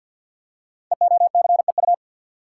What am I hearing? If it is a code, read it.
Morse code, heard as a keyed tone, about 49 wpm, 700 Hz. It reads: EOCEU
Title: Code Groups - 5 Letter:Numbers - 50wpm